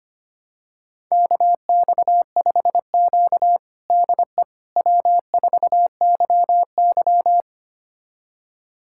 Morse code, heard as a keyed tone, about 25 wpm, 700 Hz. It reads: KX5Q DE W4YY
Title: USA Callsign DE Callsign - 4 Characters - 25wpm